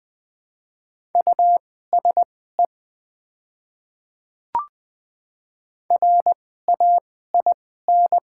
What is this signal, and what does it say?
Morse code, heard as a keyed tone, about 20 wpm, spaced as for 14 wpm, 700 Hz. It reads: USE RAIN
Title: 2x Character Spacing - Top 300 Words - 20wpm